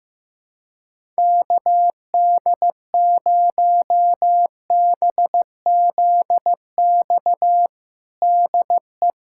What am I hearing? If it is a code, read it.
Morse code, heard as a keyed tone, about 16 wpm, 700 Hz. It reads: KD0BZX DE
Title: USA Callsign DE Callsign - 6 Characters - 15wpm